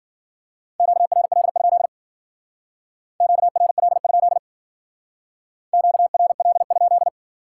Morse code, heard as a keyed tone, about 53 wpm, 700 Hz. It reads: QRL? QRL? QRL?